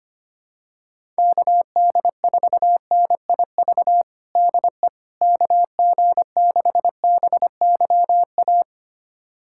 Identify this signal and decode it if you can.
Morse code, heard as a keyed tone, about 25 wpm, 700 Hz. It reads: KD4NIV DE KG6BYA
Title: USA Callsign DE Callsign 25wpm